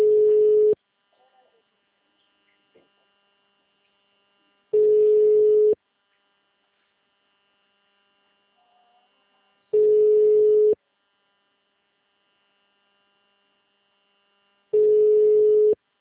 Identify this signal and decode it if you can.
Ringback tone (Europe, 425 Hz), with speech.